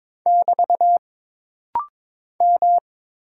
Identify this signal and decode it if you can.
Morse code, heard as a keyed tone, about 22 wpm, 700 Hz. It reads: <BT> M
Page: This rapid-fire timing is geared to push your ICR (Instant Character Recognition) skills.